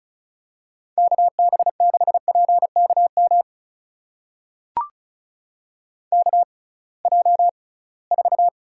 Morse code, heard as a keyed tone, about 36 wpm, spaced as for 23 wpm, 700 Hz. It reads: KB6PKM KJ4
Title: USA Call Signs - 6 Characters - 6x Character Spacing - 35wpm